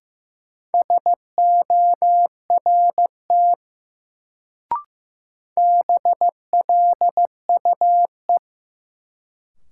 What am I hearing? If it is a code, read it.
Morse code, heard as a keyed tone, about 16 wpm, 700 Hz. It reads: SORT BLUE
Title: Sentences from Top 1000 Words - Review Part 3 - New Word Drill - 15wpm